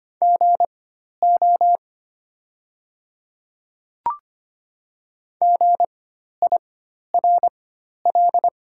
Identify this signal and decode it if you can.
Morse code, heard as a keyed tone, about 25 wpm, spaced as for 12 wpm, 700 Hz. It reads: GO GIRL
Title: 4x Character Spacing - Top 500 Words - 25wpm